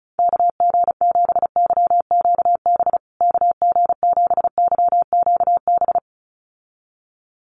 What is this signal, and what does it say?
Morse code, heard as a keyed tone, about 34 wpm, 700 Hz. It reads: KG7YQB KG7YQB
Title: USA Call Signs Repeated 2x 35wpm